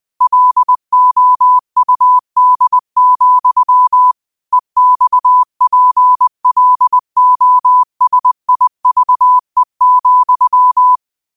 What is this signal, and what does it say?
Morse code, heard as a keyed tone, about 20 wpm, 1000 Hz. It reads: LOUD, EXPLOSIVE,